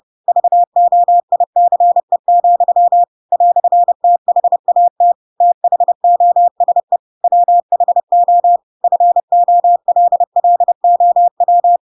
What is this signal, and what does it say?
Morse code, heard as a keyed tone, about 30 wpm, 700 Hz. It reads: VOICE, "THAT THOSE WHO FOLLOW
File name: Pinocchio / Pinocchio ch4 30 30 ik1hge